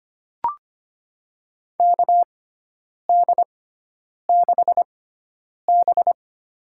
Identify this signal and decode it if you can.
Morse code, heard as a keyed tone, about 25 wpm, spaced as for 9 wpm, 700 Hz. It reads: KD6B
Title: USA Call Signs - 4 Characters - 6x Character Spacing - 25wpm